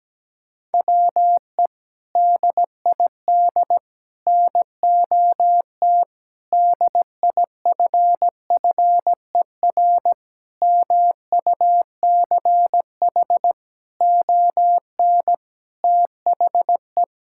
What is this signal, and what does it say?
Morse code, heard as a keyed tone, about 17 wpm, 700 Hz. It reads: WE DID NOT DIFFER MUCH ON THE
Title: Sentences from Top 1000 Words - Review Part 6 - 17wpm